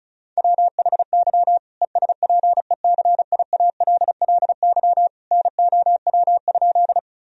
Morse code, heard as a keyed tone, about 36 wpm, 700 Hz. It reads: WHY ESPECIALLY NOW?
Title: Sentences from Top 1000 Words - Review Part 4 - 35wpm